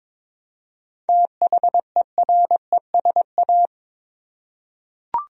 Morse code, heard as a keyed tone, about 22 wpm, 700 Hz. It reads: THERESA
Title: USA Common Names - 22wpm